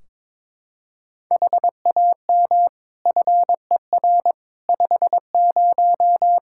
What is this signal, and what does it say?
Morse code, heard as a keyed tone, about 22 wpm, 700 Hz. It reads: HAM FER 50
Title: Practice QSO - Exchange 3 - More Info - 22wpm